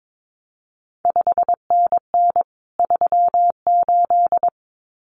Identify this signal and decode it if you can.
Morse code, heard as a keyed tone, about 22 wpm, 700 Hz. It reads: 5NN 38
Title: CQ WW DX Contest Exchange Encore - 22wpm